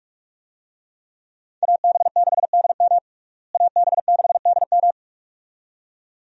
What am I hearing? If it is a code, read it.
Morse code, heard as a keyed tone, about 44 wpm, 700 Hz. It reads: AB6DM AB6DM
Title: USA Call Signs - 5 Characters - Repeated 2x - 45wpm